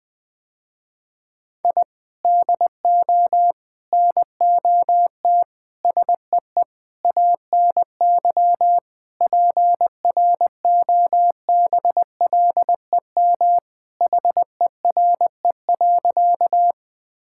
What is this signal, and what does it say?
Morse code, heard as a keyed tone, about 20 wpm, 700 Hz. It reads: I DO NOT SEE ANY PROBLEM HERE.
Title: Sentences with 5-Word Phrases - Spoken Beginning - 20wpm